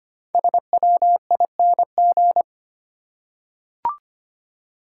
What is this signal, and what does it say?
Morse code, heard as a keyed tone, about 25 wpm, 700 Hz. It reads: SWING
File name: practice / 25wpm / Easy Words with Hints - 25wpm